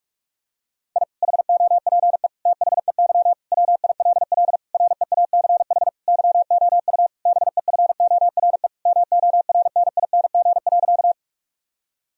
Morse code, heard as a keyed tone, about 47 wpm, 700 Hz. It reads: I HOPE THEY WILL REACH YOU BEFORE MORNING.
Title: Sentences from Top 1000 Words - Review Part 2 - 45wpm